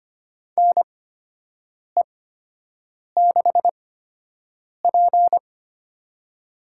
Morse code, heard as a keyed tone, about 25 wpm, spaced as for 7 wpm, 700 Hz. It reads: NE6P
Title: USA Call Signs - 4 Characters - 8x Character Spacing - 25wpm